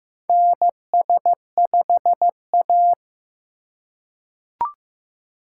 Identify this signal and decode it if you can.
Morse code, heard as a keyed tone, about 15 wpm, 700 Hz. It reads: NS5A
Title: USA Call Signs - 4 Characters - 4x Character Spacing - 15wpm